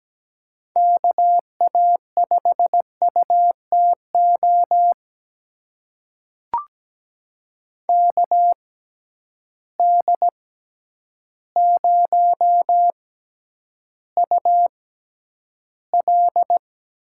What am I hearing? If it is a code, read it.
Morse code, heard as a keyed tone, about 17 wpm, spaced as for 9 wpm, 700 Hz. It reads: KA5UTO KD0UL
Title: USA Call Signs - 6 Characters - 6x Character Spacing - 17wpm